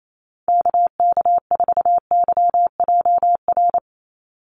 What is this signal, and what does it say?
Morse code, heard as a keyed tone, about 28 wpm, 700 Hz. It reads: KK4YJR
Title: USA Call Signs 28wpm